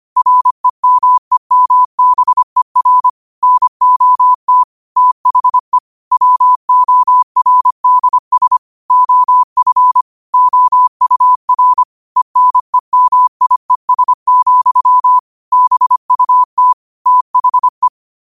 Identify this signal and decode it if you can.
Morse code, heard as a keyed tone, about 25 wpm, 1000 Hz. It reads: REMEMBER NOT THE WORDS OF OUR ENEMIES, BUT THE